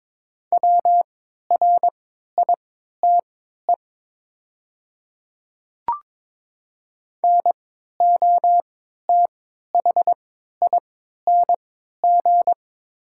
Morse code, heard as a keyed tone, about 21 wpm, spaced as for 12 wpm, 700 Hz. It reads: WRITE NOTHING
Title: 3x Character Spacing - Top 500 Words - 22wpm